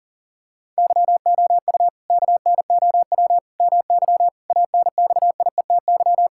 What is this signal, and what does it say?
Morse code, heard as a keyed tone, about 40 wpm, 700 Hz. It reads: YOU KNOW MY ANXIETY
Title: Sentences from Top 1600 Words 40wpm